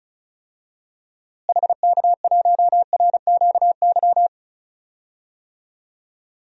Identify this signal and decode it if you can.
Morse code, heard as a keyed tone, about 34 wpm, 700 Hz. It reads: HK1RQY
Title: DXCC-100 Call Signs - 6 Characters - Repeated 2x - 35wpm